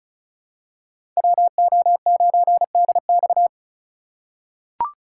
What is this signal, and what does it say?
Morse code, heard as a keyed tone, about 36 wpm, 700 Hz. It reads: WO9DX
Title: USA Call Signs - 5 Characters - 2x Character Spacing - 35wpm